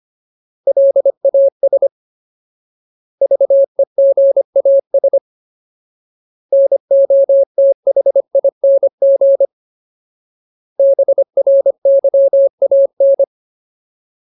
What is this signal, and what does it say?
Morse code, heard as a keyed tone, about 25 wpm, 550 Hz. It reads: LAS VEGAS NOTHING BRYAN
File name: Ultimate Word List / 25WPM Ultimate